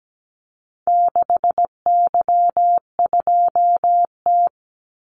Morse code, heard as a keyed tone, about 17 wpm, 700 Hz. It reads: 6Y2T
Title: Call Signs Repeated 3x - 17wpm